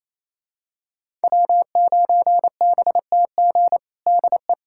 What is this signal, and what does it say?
Morse code, heard as a keyed tone, about 28 wpm, 700 Hz. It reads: W9BTG DE
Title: USA Callsign DE Callsign - 5 Characters - 28wpm